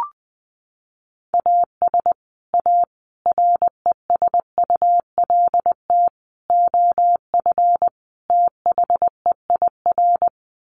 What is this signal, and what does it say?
Morse code, heard as a keyed tone, about 20 wpm, 700 Hz. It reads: AS A RESULT OF THEIR
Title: Sets of 5 Words Encore2 20wpm